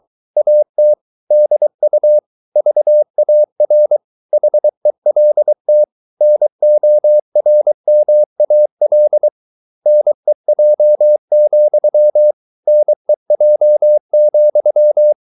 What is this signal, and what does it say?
Morse code, heard as a keyed tone, about 24 wpm, 600 Hz. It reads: AT DU VAR HELT NORMAL NEJ, NEJ,
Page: Ferie 23 wpm